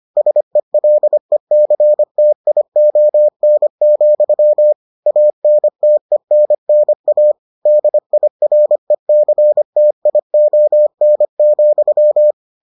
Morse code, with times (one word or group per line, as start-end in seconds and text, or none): 0.2-4.7 SELECTION,
5.1-7.3 ANTENNA
7.7-12.3 DIRECTION,